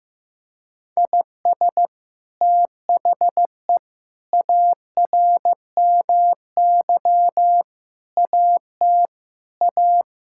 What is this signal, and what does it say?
Morse code, heard as a keyed tone, about 15 wpm, 700 Hz. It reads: IS THE ARMY AT A
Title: Sentences from Top 1000 Words - Review Part 6 - 15wpm